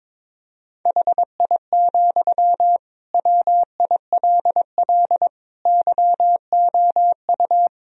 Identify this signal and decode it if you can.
Morse code, heard as a keyed tone, about 22 wpm, 700 Hz. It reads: HI, WILL YOU
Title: Sentences from Top 1000 Words - Review Part 3 - 22wpm